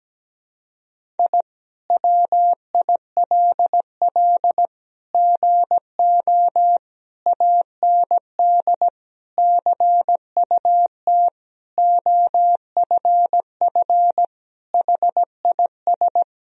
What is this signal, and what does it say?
Morse code, heard as a keyed tone, about 17 wpm, 700 Hz. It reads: I WILL GO AND CUT OFF HIS
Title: Sentences from Top 1000 Words - Review Part 1 - 17wpm